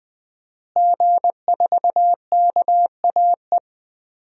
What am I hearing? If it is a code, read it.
Morse code, heard as a keyed tone, about 20 wpm, 700 Hz. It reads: G4KAE
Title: DXCC-100 Call Signs - 5 Characters - Repeated 2x - 20wpm